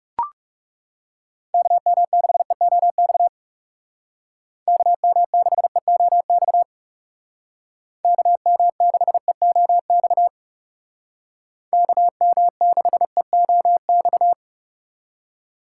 Morse code, heard as a keyed tone, about 39 wpm, 700 Hz. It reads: KM6EOX KM6EOX KM6EOX KM6EOX